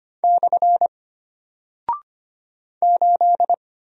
Morse code, heard as a keyed tone, about 25 wpm, 700 Hz. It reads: / 8
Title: Single Character (Letter:Number:Punctuation:Prosign) Alt 25wpm